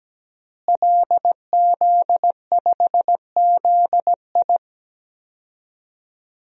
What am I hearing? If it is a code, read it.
Morse code, heard as a keyed tone, about 17 wpm, 700 Hz. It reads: LZ5ZI
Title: DXCC-100 Call Signs - 5 Characters - Repeated 2x - 17wpm